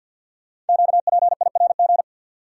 Morse code, heard as a keyed tone, about 52 wpm, 700 Hz. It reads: QPIRG
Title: Code Groups - 5 Letters - 50wpm